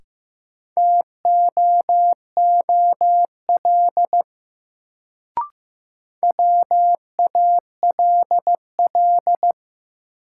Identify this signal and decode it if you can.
Morse code, heard as a keyed tone, about 15 wpm, 700 Hz. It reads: TOOL WALL
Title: Sentences from Top 1000 Words - Review Part 6 - New Word Drill - 15wpm